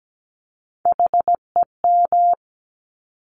Morse code, heard as a keyed tone, about 17 wpm, 700 Hz. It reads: HEM
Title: Top Three Letter Combinations in Words 17wpm